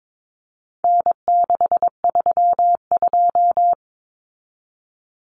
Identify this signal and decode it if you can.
Morse code, heard as a keyed tone, about 22 wpm, 700 Hz. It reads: N632
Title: Common Cut Numbers - 4-Digit 22wpm